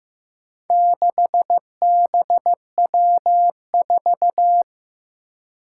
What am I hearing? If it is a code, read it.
Morse code, heard as a keyed tone, about 15 wpm, 700 Hz. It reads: 6BW4
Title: Four Challenging Letter-Number Combinations 15wpm